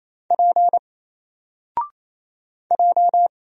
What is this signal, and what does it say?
Morse code, heard as a keyed tone, about 28 wpm, 700 Hz. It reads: P J
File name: Single-Letter-Number-28wpm